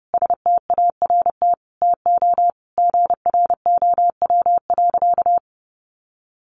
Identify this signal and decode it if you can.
Morse code, heard as a keyed tone, about 30 wpm, 700 Hz. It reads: START TO GROW.
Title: Easy Phrases 30wpm